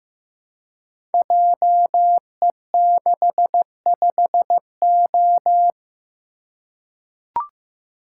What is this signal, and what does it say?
Morse code, heard as a keyed tone, about 15 wpm, 700 Hz. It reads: JE65O
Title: Code Groups - 5 Letter:Numbers - 15wpm